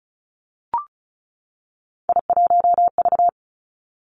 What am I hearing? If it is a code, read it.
Morse code, heard as a keyed tone, about 36 wpm, 700 Hz. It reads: I1V
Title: Three Challenging Letter-Number Combinations 35wpm